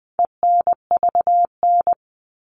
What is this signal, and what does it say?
Morse code, heard as a keyed tone, about 20 wpm, 700 Hz. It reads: ENVN
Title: All Cut Numbers - 4-Digit 20wpm